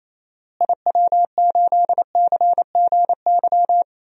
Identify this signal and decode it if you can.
Morse code, heard as a keyed tone, about 27 wpm, 700 Hz. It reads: IW8CGY
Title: DXCC-100 Call Signs - 6 Characters - Repeated 3x - 28wpm